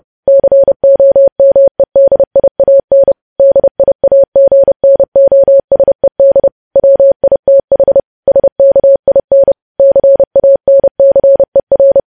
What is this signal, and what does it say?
Morse code, heard as a keyed tone, about 30 wpm, 550 Hz. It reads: COMEDIAN DIAGNOSED WITH SKIN CANCER
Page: news headlines in morse code at 30 wpm